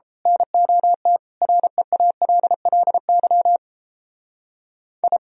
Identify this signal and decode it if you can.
Morse code, heard as a keyed tone, about 30 wpm, 700 Hz. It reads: NOT REALLY I
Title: Speed-Racing - Sets of 4 Words 28wpm